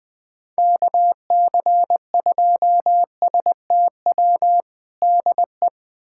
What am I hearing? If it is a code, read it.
Morse code, heard as a keyed tone, about 20 wpm, 700 Hz. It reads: KC2STW DE
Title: USA Callsign DE Callsign - 6 Characters - 20wpm